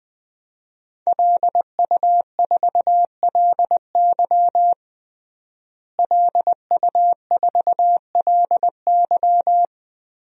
Morse code, heard as a keyed tone, about 20 wpm, 700 Hz. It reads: LU4LY LU4LY
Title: DXCC-100 Call Signs - 5 Characters - Repeated 3x - 20wpm